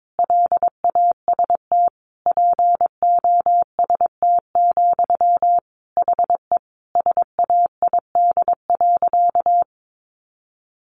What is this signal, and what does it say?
Morse code, heard as a keyed tone, about 22 wpm, 700 Hz. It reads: LAST POST, HE SAID.
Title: Sentences from Top 1000 Words 22wpm